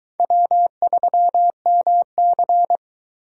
Morse code, heard as a keyed tone, about 22 wpm, 700 Hz. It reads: W3MC